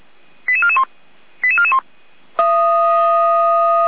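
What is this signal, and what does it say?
Five-tone selective call (ZVEI), heard as quick runs of five tones, tones of about 70 ms. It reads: 88441 88441
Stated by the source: ZVEI five-tone call 88517 received by SDR, Feuerwehr Orlamünde fire brigade audio server.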